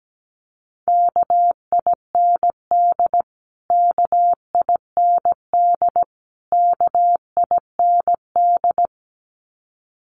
Morse code, heard as a keyed tone, about 17 wpm, 700 Hz. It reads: KIND KIND KIND
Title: Top 500 Words Repeated 17wpm